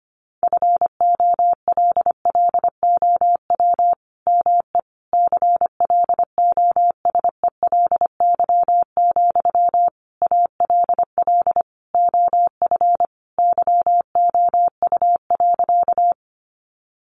Morse code, heard as a keyed tone, about 25 wpm, 700 Hz. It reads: FOLLOW ME CLOSELY, ALL OF YOU.
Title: Sentences from Top 1400 Words 25wpm